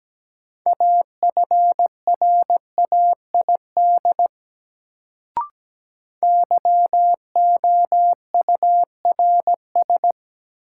Morse code, heard as a keyed tone, about 17 wpm, 700 Hz. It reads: AFRAID YOURS
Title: Sentences from Top 1500 Words - New Word Drill - 17wpm